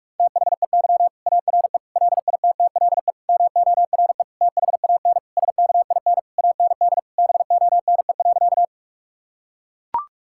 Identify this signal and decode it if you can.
Morse code, heard as a keyed tone, about 43 wpm, 700 Hz. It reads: THEY ARE LITTLE MORE THAN SKIN AND BONE.
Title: Sentences from Top 1600 Words 45wpm